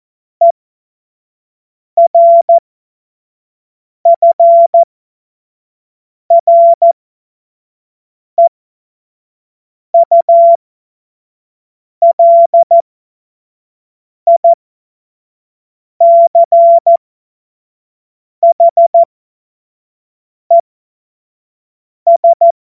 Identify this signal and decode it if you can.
Morse code, heard as a keyed tone, about 14 wpm, spaced as for 5 wpm, 680 Hz. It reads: ERFREULICHES
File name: Witz-16-September-2025-25-F70